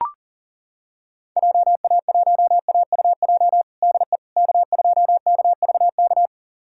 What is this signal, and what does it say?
Morse code, heard as a keyed tone, about 40 wpm, 700 Hz. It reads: JA1AUJ DE K2KVK
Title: USA Callsign DE Callsign 40wpm